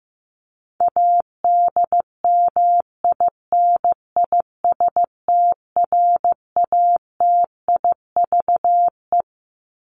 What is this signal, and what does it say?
Morse code, heard as a keyed tone, about 15 wpm, 700 Hz. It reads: ADMINISTRATIVE
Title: Top 1501-1750 Words 15wpm